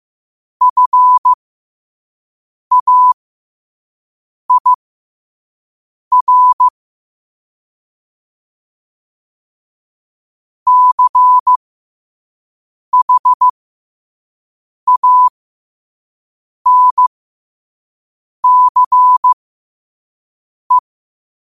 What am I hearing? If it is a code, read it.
Morse code, heard as a keyed tone, about 15 wpm, spaced as for 5 wpm, 1000 Hz. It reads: FAIR CHANCE
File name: 5wpm